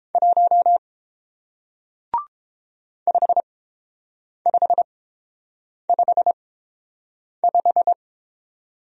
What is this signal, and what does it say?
Morse code, heard as a keyed tone, about 31 wpm, spaced as for 7 wpm, 700 Hz. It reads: 1 5555